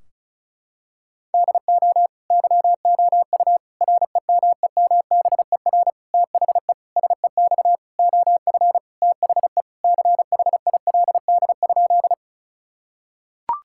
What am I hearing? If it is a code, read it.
Morse code, heard as a keyed tone, about 36 wpm, 700 Hz. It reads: DO YOU REMEMBER THE SEX OF THE CHILD?
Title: Sentences from Top 1600 Words 35wpm